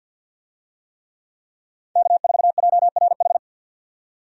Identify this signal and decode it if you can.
Morse code, heard as a keyed tone, about 48 wpm, 700 Hz. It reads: K4JRH
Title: USA Call Signs - 5 Characters - 50wpm